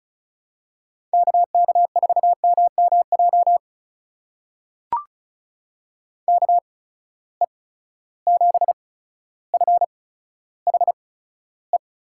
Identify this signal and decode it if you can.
Morse code, heard as a keyed tone, about 36 wpm, spaced as for 15 wpm, 700 Hz. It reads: KK4MMJ KE7FHE
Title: USA Call Signs - 6 Characters - 8x Character Spacing - 35wpm